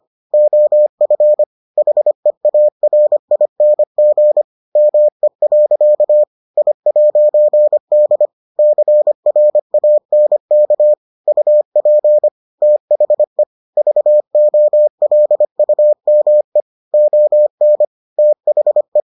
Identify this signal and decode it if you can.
Morse code, heard as a keyed tone, about 25 wpm, 600 Hz. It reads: OF HEARING ME. I'D CRANK UP THE VOLUME ON THE